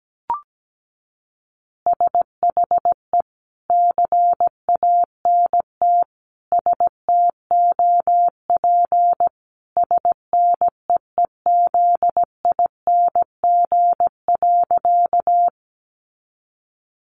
Morse code, heard as a keyed tone, about 17 wpm, 700 Hz. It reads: SHE CANT STOP SNEEZING.
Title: Easy Phrases 17wpm